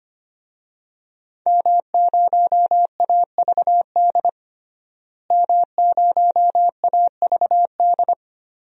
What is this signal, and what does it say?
Morse code, heard as a keyed tone, about 25 wpm, 700 Hz. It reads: M0AVD M0AVD
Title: DXCC-100 Call Signs - 5 Characters - Repeated 2x - 25wpm